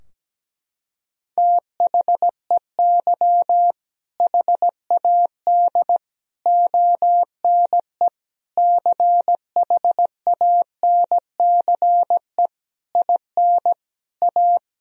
Morse code, heard as a keyed tone, about 17 wpm, 700 Hz. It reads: THEY HAD ONE CHANCE IN A
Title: Sentences from Top 1000 Words - Review Part 3 - 17wpm